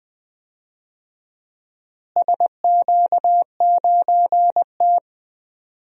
Morse code, heard as a keyed tone, about 20 wpm, 700 Hz. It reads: SQ9T
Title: DXCC-100 Call Signs - 4 Characters - Repeated 2x - 20wpm